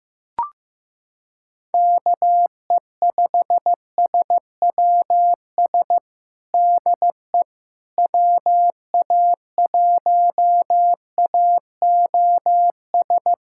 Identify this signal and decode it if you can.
Morse code, heard as a keyed tone, about 16 wpm, 700 Hz. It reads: KE5SWS DE WA1AOS
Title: USA Callsign DE Callsign - 6 Characters - 15wpm